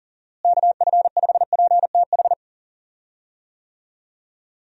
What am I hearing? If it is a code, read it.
Morse code, heard as a keyed tone, about 40 wpm, 700 Hz. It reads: KF5PTH
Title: USA Call Signs - 6 Characters - Repeated 2x - 40wpm